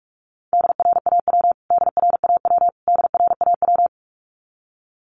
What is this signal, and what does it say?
Morse code, heard as a keyed tone, about 44 wpm, 700 Hz. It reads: DRAW DRAW DRAW
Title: Top 500 Words Repeated 45wpm